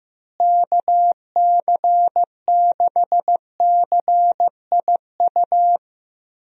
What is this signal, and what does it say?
Morse code, heard as a keyed tone, about 15 wpm, 700 Hz. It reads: KC6CIU
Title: USA Call Signs - 6 Characters - Repeated 3x - 15wpm